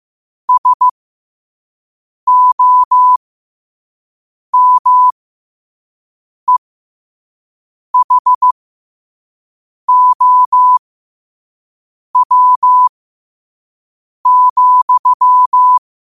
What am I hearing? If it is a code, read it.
Morse code, heard as a keyed tone, about 15 wpm, spaced as for 5 wpm, 1000 Hz. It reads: SOMEHOW,
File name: qotdfiles / 5wpm